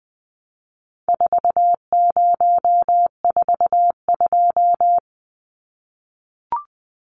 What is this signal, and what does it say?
Morse code, heard as a keyed tone, about 20 wpm, 700 Hz. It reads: 4042
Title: Numbers - 4-Digit 20wpm